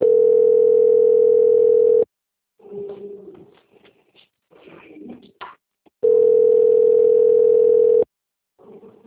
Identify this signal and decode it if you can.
Ringback tone (North America), 440 + 480 Hz.